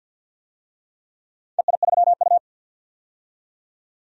Morse code, heard as a keyed tone, about 48 wpm, 700 Hz. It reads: EI3U
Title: DXCC-100 Call Signs - 4 Characters - 50wpm